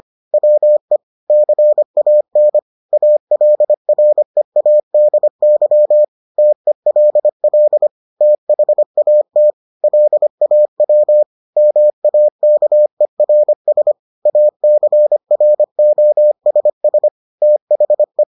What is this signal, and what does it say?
Morse code, heard as a keyed tone, about 25 wpm, 600 Hz. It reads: WE CAN ALREADY TELL THAT LAW MAKERS ACROSS THE